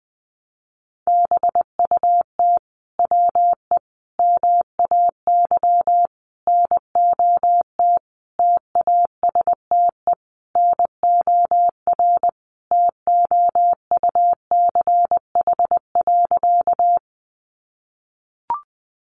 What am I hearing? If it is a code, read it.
Morse code, heard as a keyed tone, about 20 wpm, 700 Hz. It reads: BUT WE MAY NOT TASTE NOR TOUCH.
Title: Sentences from Top 1500 Words 20wpm